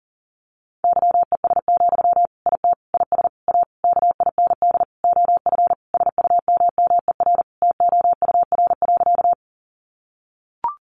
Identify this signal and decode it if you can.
Morse code, heard as a keyed tone, about 40 wpm, 700 Hz. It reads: YES, IT IS A KIND OF SUMMER TOUR.
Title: Sentences from Top 1400 Words 40wpm